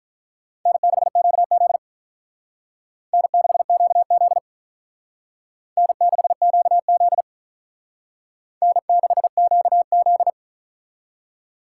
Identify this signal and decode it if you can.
Morse code, heard as a keyed tone, about 43 wpm, 700 Hz. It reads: N6QZ N6QZ N6QZ N6QZ
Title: Speed Racing - USA 1x2 Call Signs 35wpm